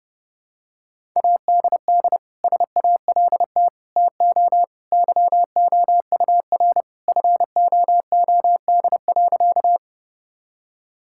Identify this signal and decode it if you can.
Morse code, heard as a keyed tone, about 32 wpm, 700 Hz. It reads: ADD SALT TO YOUR FOOD.
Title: Sentences from Top 1000 Words - Review Part 5 - 30wpm